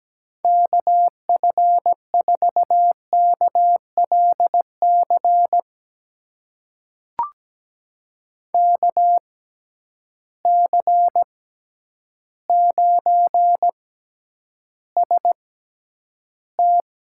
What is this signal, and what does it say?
Morse code, heard as a keyed tone, about 17 wpm, spaced as for 9 wpm, 700 Hz. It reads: KF4KLC KC9ST
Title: USA Call Signs - 6 Characters - 6x Character Spacing - 17wpm